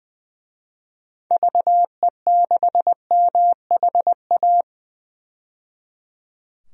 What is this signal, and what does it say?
Morse code, heard as a keyed tone, about 20 wpm, 700 Hz. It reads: VE6MHA
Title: DXCC-100 Call Signs - 6 Characters - 20wpm